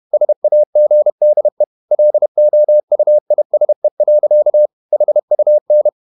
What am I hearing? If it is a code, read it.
Morse code, heard as a keyed tone, about 31 wpm, 600 Hz. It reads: SAGDE LOUISE. HUN
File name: Ferie QRQ 31wpm